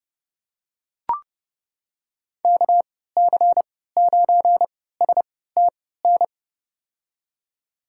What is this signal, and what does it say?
Morse code, heard as a keyed tone, about 30 wpm, spaced as for 17 wpm, 700 Hz. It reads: KC9STN
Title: USA Call Signs - 6 Characters - 3x Character Spacing - 30wpm